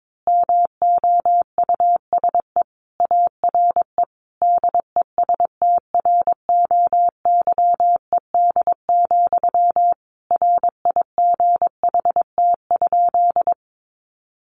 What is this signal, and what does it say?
Morse code, heard as a keyed tone, about 22 wpm, 700 Hz. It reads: MOUSE ARE DESTROYED, RIGHT?
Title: A Sound Of Thunder 22wpm